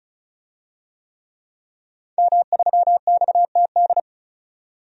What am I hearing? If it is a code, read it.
Morse code, heard as a keyed tone, about 35 wpm, 700 Hz. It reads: M3XTD
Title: DXCC-100 Call Signs - 5 Characters - 35wpm